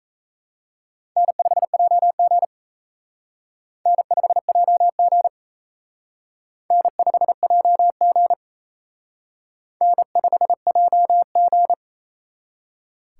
Morse code, heard as a keyed tone, about 34 wpm, 700 Hz. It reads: N5JG N5JG N5JG N5JG
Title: Speed Racing - USA 1x2 Call Signs 28wpm